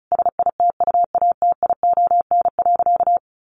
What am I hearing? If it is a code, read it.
Morse code, heard as a keyed tone, about 34 wpm, 700 Hz. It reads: SITUATION.
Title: Sentences from Top 1200 Words 35wpm